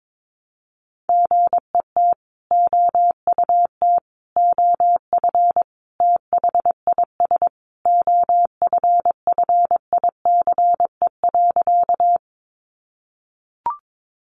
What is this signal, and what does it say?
Morse code, heard as a keyed tone, about 22 wpm, 700 Hz. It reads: GET OUT OF THIS OFFICE.
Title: Sentences from Top 1000 Words 22wpm